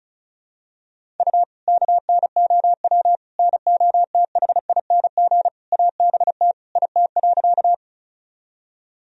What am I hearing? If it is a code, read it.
Morse code, heard as a keyed tone, about 34 wpm, 700 Hz. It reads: U KNOW NOTHING ABT IT.
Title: Sentences from Top 500 Words and Abbreviations 35wpm